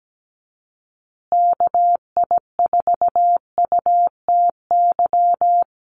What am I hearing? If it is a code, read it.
Morse code, heard as a keyed tone, about 17 wpm, 700 Hz. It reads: KI4UTY
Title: USA Call Signs Repeated 2x 17wpm